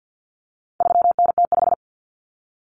Morse code, heard as a keyed tone, about 50 wpm, 700 Hz. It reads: 3NT5
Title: Common Cut Numbers - 4-Digit 50wpm